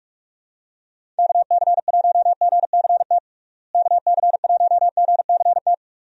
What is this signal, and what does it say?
Morse code, heard as a keyed tone, about 46 wpm, 700 Hz. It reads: KC1GCT KC1GCT
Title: USA Call Signs - 6 Characters - Repeated 3x - 45wpm